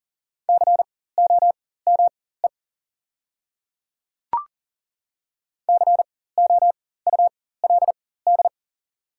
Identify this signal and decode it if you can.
Morse code, heard as a keyed tone, about 40 wpm, spaced as for 19 wpm, 700 Hz. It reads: COME COULD
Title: 4x Character Spacing - Top 200 Words - 40wpm